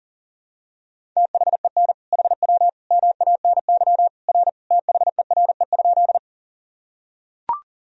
Morse code, heard as a keyed tone, about 40 wpm, 700 Hz. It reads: THEN HW MANY R THERE?
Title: Sentences from Top 100 Words and Abbreviations 40wpm